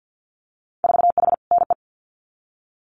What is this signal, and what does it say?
Morse code, heard as a keyed tone, about 50 wpm, 700 Hz. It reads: 4H NE
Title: Winter Field Day Exchanges Encore 50wpm